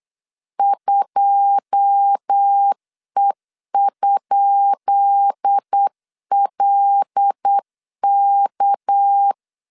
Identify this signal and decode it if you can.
Morse code, heard as a keyed tone, about 8 wpm, 800 Hz. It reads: 2E?LK
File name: LEZIONI / 10 SIGNS / 10D COMA 8wpm 3749